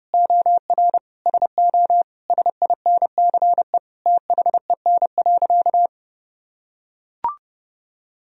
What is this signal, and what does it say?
Morse code, heard as a keyed tone, about 30 wpm, 700 Hz. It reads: OR SO SINCE THEN.
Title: Sentences from Top 1000 Words - Review Part 1 - 30wpm